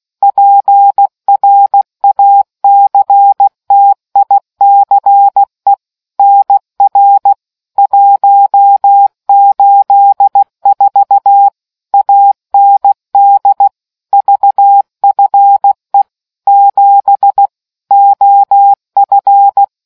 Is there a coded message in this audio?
Yes: Morse code from 0.2 to 19.6 s, heard as a keyed tone, about 16 wpm, 800 Hz. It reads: PRACTICE NR 184 AND VFE 7 OF